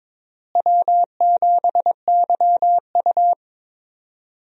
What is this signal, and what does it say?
Morse code, heard as a keyed tone, about 22 wpm, 700 Hz. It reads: W7YU
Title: USA Call Signs - 4 Characters - Repeated 3x - 22wpm